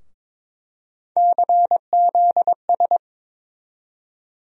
Morse code, heard as a keyed tone, about 22 wpm, 700 Hz. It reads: CZS
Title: USA Call Sign Postfixes - 3 Letters 22wpm